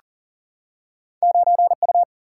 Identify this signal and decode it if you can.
Morse code, heard as a keyed tone, about 40 wpm, 700 Hz. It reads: 9U
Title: Single-Letter Single-Number 40wpm